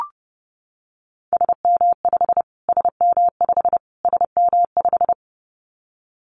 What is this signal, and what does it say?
Morse code, heard as a keyed tone, about 30 wpm, 700 Hz. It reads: SM5 SM5 SM5
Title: Callsign Prefixes for Top 25 DXCC - Repeated 3x - 30wpm